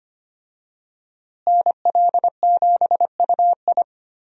Morse code, heard as a keyed tone, about 25 wpm, 700 Hz. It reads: NL7UI
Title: USA Call Signs - 5 Characters - Repeated 2x - 25wpm